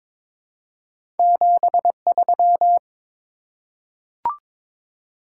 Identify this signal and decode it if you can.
Morse code, heard as a keyed tone, about 22 wpm, 700 Hz. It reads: 73
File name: practice / 22wpm / Top 20 QSO Elements - 22wpm